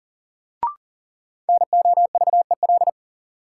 Morse code, heard as a keyed tone, about 40 wpm, 700 Hz. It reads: NOVEL
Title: Sentences from Top 1300 Words - New Word Drill - 40wpm